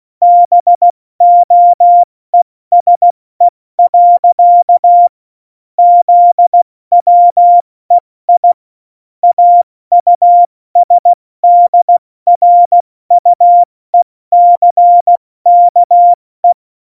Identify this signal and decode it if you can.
Morse code, heard as a keyed tone, about 16 wpm, 700 Hz. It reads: BOESE. ZWEI AUSDRUECKE
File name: Morsekurs / Witze / Witz 08. August 17, Tempo 70 F80